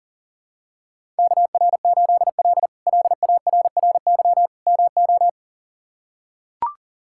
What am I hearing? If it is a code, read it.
Morse code, heard as a keyed tone, about 37 wpm, 700 Hz. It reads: KR8L LARRY MO
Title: K1USN SST QSO Practice - 40wpm